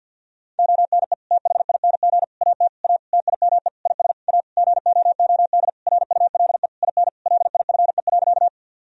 Morse code, heard as a keyed tone, about 51 wpm, 700 Hz. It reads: ONE THING AT A TIME IS A GOOD RULE IN LIFE.
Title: Sentences from Top 1000 Words - Review Part 2 - 50wpm